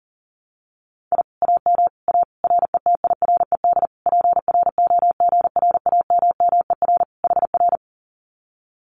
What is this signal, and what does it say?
Morse code, heard as a keyed tone, about 40 wpm, 700 Hz. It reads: I AM A RETIRED PROGRAMMER HR
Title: Common QSO Phrases Encore 40wpm